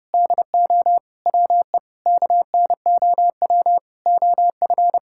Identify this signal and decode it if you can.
Morse code, heard as a keyed tone, about 30 wpm, 700 Hz. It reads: DO WE KNOW OF
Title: Sentences from Top 1600 Words 30wpm